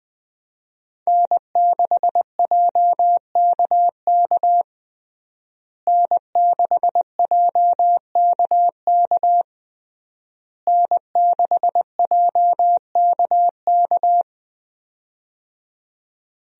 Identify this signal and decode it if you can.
Morse code, heard as a keyed tone, about 20 wpm, 700 Hz. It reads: N6JKK N6JKK N6JKK
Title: USA Call Signs - 5 Characters - Repeated 3x - 20wpm